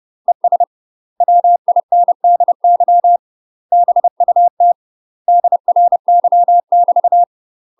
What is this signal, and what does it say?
Morse code, heard as a keyed tone, about 30 wpm, 700 Hz. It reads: ES WINDY BUT DRY <BT>
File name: UK MASTER QSOs TRAINING / 30WPM / UK MASTER QSOs 0014